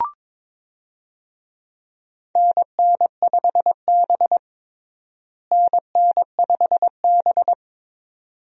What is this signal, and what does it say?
Morse code, heard as a keyed tone, about 22 wpm, 700 Hz. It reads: NN5B NN5B
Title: USA Call Signs - 4 Characters - Repeated 2x - 22wpm